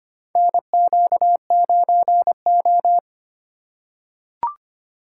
Morse code, heard as a keyed tone, about 25 wpm, 700 Hz. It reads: NQ9O
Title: USA Call Signs - 3x Character Spacing - 25wpm